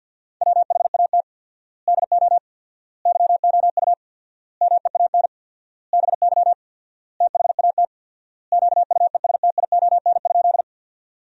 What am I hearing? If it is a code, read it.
Morse code, heard as a keyed tone, about 50 wpm, 700 Hz. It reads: WHAT DO YOU MEAN BY THAT QUESTION?
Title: 4x Word-Spacing - Sentences from Top 400 Words - 50wpm